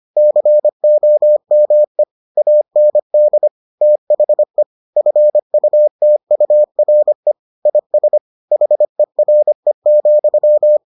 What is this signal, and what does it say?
Morse code, heard as a keyed tone, about 25 wpm, 600 Hz. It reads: COME AND THE FUTURE IS HERE,